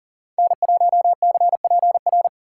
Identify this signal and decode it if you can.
Morse code, heard as a keyed tone, about 48 wpm, 700 Hz. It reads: N1CPR
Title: Speed Racing - USA 1x3 Call Signs 40wpm